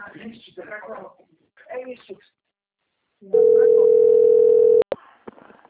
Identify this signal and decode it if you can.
Ringback tone (North America), 440 + 480 Hz, with speech.
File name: out- - - - - .2038